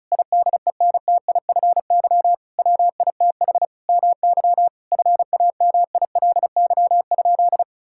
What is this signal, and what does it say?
Morse code, heard as a keyed tone, about 36 wpm, 700 Hz. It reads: IDENTIFY WITH MY FAMILY?
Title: Sentences from Top 1000 Words - Review Part 6 - 35wpm